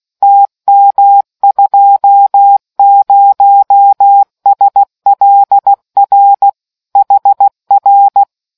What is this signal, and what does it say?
Morse code, heard as a keyed tone, about 16 wpm, 800 Hz. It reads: TM20SLR HR